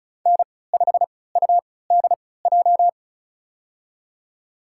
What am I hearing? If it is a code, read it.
Morse code, heard as a keyed tone, about 36 wpm, spaced as for 20 wpm, 700 Hz. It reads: N5UDJ
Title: USA Call Signs - 5 Characters - 3x Character Spacing - 35wpm